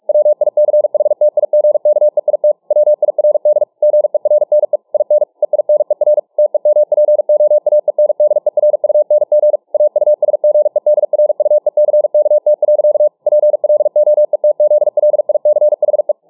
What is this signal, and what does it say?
Morse code, heard as a keyed tone, about 47 wpm, 600 Hz. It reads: WICHTIGKEIT WIRD GERNE IN EINER TEMPOAENDERUNG AUSGEDRUECKT. PLOETZLICHE